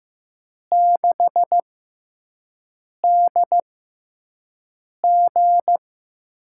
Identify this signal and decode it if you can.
Morse code, heard as a keyed tone, about 15 wpm, spaced as for 5 wpm, 700 Hz. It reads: 6DG
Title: USA Call Signs - 5 Characters - 6x Character Spacing - 15wpm